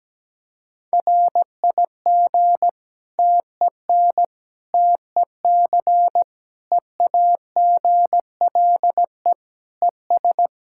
Morse code, heard as a keyed tone, about 17 wpm, 700 Hz. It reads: RIG TEN TEC EAGLE ES
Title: K1USN SST QSO Practice 17wpm